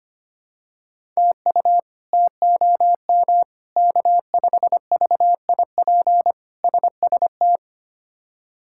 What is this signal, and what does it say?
Morse code, heard as a keyed tone, about 25 wpm, 700 Hz. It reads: TU TOM K5VIP SST
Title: K1USN SST QSO Practice - 25wpm